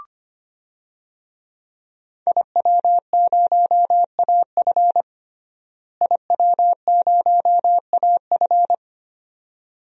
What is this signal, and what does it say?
Morse code, heard as a keyed tone, about 25 wpm, 700 Hz. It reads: IW0AF IW0AF
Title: DXCC-100 Call Signs - 5 Characters - Repeated 2x - 25wpm